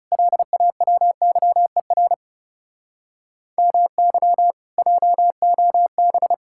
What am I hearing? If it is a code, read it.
Morse code, heard as a keyed tone, about 37 wpm, 700 Hz. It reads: LAWYER MY JOB
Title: Speed-Racing Common QSO Phrases - 30wpm